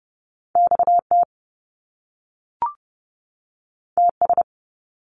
Morse code, heard as a keyed tone, about 30 wpm, 700 Hz. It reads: XT TS
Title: Top Two Letter Combinations in Words 30wpm